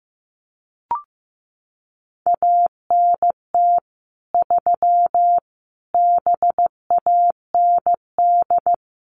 Morse code, heard as a keyed tone, about 15 wpm, 700 Hz. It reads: ANT 3 BAND
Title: Common QSO Phrases 15wpm